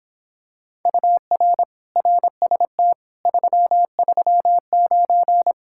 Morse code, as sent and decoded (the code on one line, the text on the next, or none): ..- .-. / .-. ... - / ...-- ...-- ----.
UR RST 339